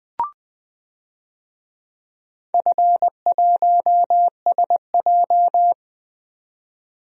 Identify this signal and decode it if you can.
Morse code, heard as a keyed tone, about 20 wpm, 700 Hz. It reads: F1SJ
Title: DXCC-100 Call Signs - 4 Characters - 20wpm